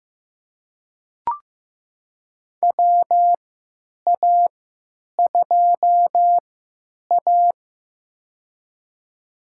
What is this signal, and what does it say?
Morse code, heard as a keyed tone, about 15 wpm, spaced as for 8 wpm, 700 Hz. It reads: WA2A
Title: USA Call Signs - 3x Character Spacing - 15wpm